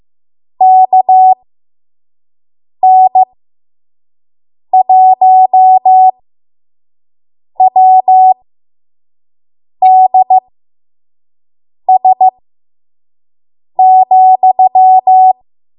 Morse code, heard as a keyed tone, about 15 wpm, spaced as for 5 wpm, 750 Hz. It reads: KN1WDS,
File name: Morse / 05WPM